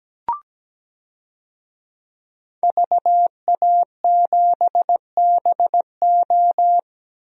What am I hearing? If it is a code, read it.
Morse code, heard as a keyed tone, about 17 wpm, 700 Hz. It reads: VA7BO
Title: DXCC-100 Call Signs - 5 Characters - 17wpm